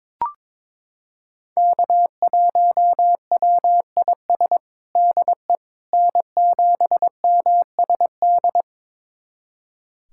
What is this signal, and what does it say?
Morse code, heard as a keyed tone, about 22 wpm, 700 Hz. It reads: K1WIS DE N7MSD
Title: USA Callsign DE Callsign 22wpm